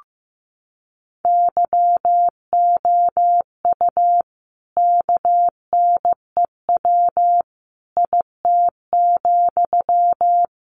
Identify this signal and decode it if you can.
Morse code, heard as a keyed tone, about 15 wpm, 700 Hz. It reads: YOU KNEW IT,
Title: Sentences from Top 500 Words 15wpm